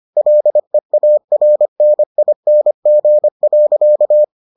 Morse code, heard as a keyed tone, about 25 wpm, 600 Hz. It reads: LEARNING.